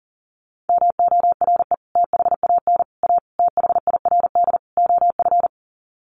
Morse code, heard as a keyed tone, about 40 wpm, 700 Hz. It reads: MORE THAN A THIRD OF
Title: Sets of 5 Words Encore2 40wpm